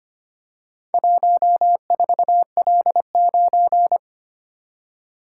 Morse code, heard as a keyed tone, about 25 wpm, 700 Hz. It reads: 14L9
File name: practice / 25wpm / Four Letter-Number Combinations - 25wpm